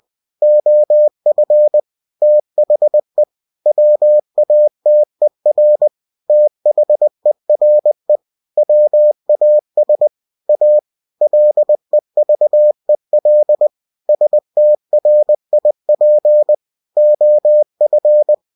Morse code, heard as a keyed tone, about 20 wpm, 600 Hz. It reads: OF THE WATER THERE WAS A LEVEL STRIP OF
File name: cw-books / winnie-the-pooh / winnie-20wpm-page-0083